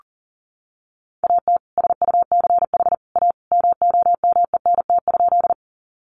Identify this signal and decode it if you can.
Morse code, heard as a keyed tone, about 40 wpm, 700 Hz. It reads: AT SUCH A MOMENT?
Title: Sentences from Top 1000 Words 40wpm